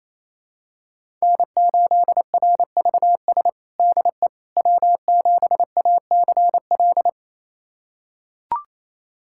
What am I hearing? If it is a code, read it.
Morse code, heard as a keyed tone, about 27 wpm, 700 Hz. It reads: N8RVS DE W7ACL
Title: USA Callsign DE Callsign - 5 Characters - 28wpm